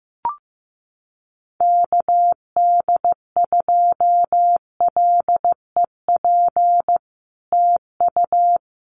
Morse code, heard as a keyed tone, about 15 wpm, 700 Hz. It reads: KD2LEP TU
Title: Common QSO Phrases Encore 15wpm